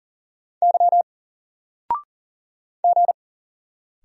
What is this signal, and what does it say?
Morse code, heard as a keyed tone, about 38 wpm, 700 Hz. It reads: Y G
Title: Single Letter-Number Alt 40wpm